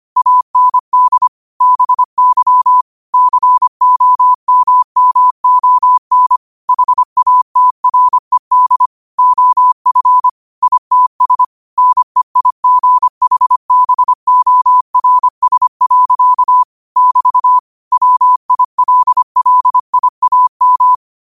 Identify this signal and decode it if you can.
Morse code, heard as a keyed tone, about 25 wpm, 1000 Hz. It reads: AND BY COMMON HATRED OF ITS NEIGHBORS. <BT> WILLIAM